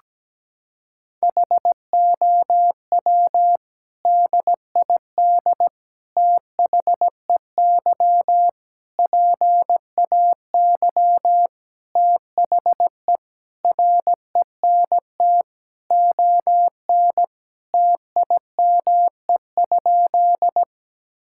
Morse code, heard as a keyed tone, about 17 wpm, 700 Hz. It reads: HOW DID THEY PAY THE RENT ON TIME?
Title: Sentences from Top 1000 Words - Review Part 5 - 17wpm